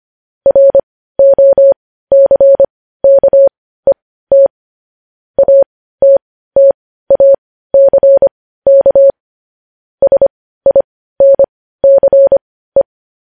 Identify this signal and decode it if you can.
Morse code, heard as a keyed tone, about 25 wpm, spaced as for 15 wpm, 550 Hz. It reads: ROCKET ATTACK SINCE
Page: news headlines in morse code at 15 wpm